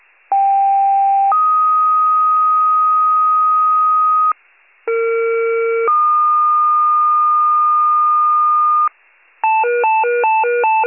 Two-tone page, heard as a one-second tone, then a three-second tone, about 790 then 1250 Hz, about 460 then 1220 Hz. It is heard from 0.3 to 8.9 s.